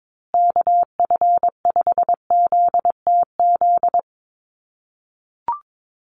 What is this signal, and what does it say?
Morse code, heard as a keyed tone, about 22 wpm, 700 Hz. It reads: KF5ZTZ
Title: USA Call Signs Repeated 3x 22wpm